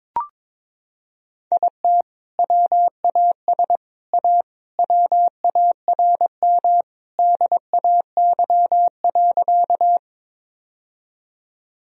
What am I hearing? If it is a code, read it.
Morse code, heard as a keyed tone, about 21 wpm, 700 Hz. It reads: IT WAS A WARM DAY.
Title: Sentences from Top 1000 Words - Review Part 2 - 22wpm